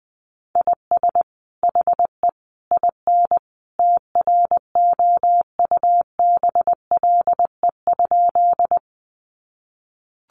Morse code, heard as a keyed tone, about 20 wpm, 700 Hz. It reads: IS HE IN TROUBLE?
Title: Sentences from Top 1200 Words 20wpm